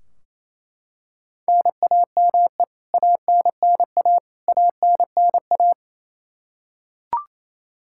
Morse code, heard as a keyed tone, about 27 wpm, 700 Hz. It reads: NAME ANNA ANNA
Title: Practice QSO - Exchange 1 - Beginning - 28wpm